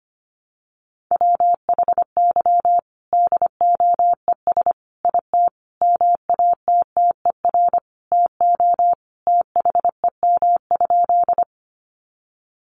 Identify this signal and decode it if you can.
Morse code, heard as a keyed tone, about 25 wpm, 700 Hz. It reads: WHY DOES IT MATTER TO THEM?
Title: Sentences from Top 700 Words 25wpm